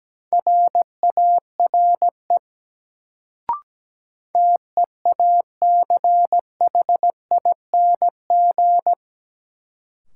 Morse code, heard as a keyed tone, about 17 wpm, 700 Hz. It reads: RARE TEACHING
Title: Sentences from Top 1100 Words - New Word Drill - 17wpm